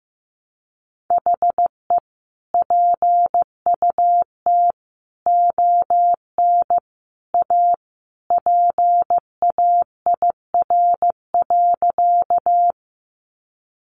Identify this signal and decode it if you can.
Morse code, heard as a keyed tone, about 15 wpm, 700 Hz. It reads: HE PUT ON A PAIR.
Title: Sentences from Top 1200 Words 15wpm